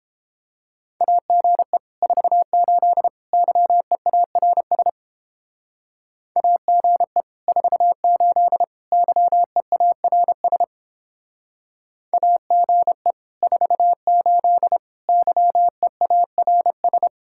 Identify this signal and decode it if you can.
Morse code, heard as a keyed tone, about 29 wpm, 700 Hz. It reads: AGE 48 YEARS AGE 48 YEARS AGE 48 YEARS